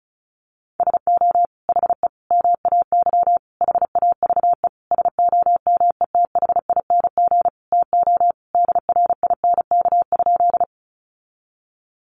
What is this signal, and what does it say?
Morse code, heard as a keyed tone, about 34 wpm, 700 Hz. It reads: SO HE MAY HAVE SOMETHING TO DRINK?
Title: Sentences from Top 1500 Words 35wpm